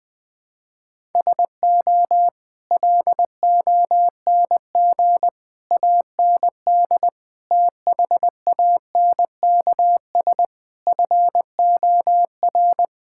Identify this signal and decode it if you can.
Morse code, heard as a keyed tone, about 20 wpm, 700 Hz. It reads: SO LONG AND THANKS FOR
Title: Sets of 8 Words - 20wpm